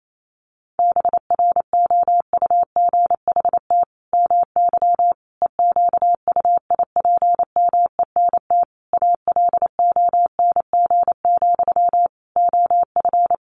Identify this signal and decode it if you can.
Morse code, heard as a keyed tone, about 28 wpm, 700 Hz. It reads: BROUGHT MY EQUIPMENT ALONG, OF
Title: Sentences from Top 700 Words 28wpm